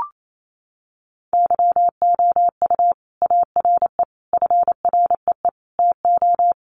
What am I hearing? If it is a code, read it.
Morse code, heard as a keyed tone, about 28 wpm, 700 Hz. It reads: YOU ARE FREE TO
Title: Sentences from Top 700 Words 28wpm